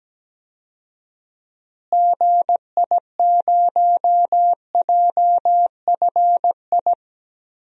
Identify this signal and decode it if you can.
Morse code, heard as a keyed tone, about 17 wpm, 700 Hz. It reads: GI0JFI
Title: DXCC-100 Call Signs - 6 Characters - Repeated 3x - 17wpm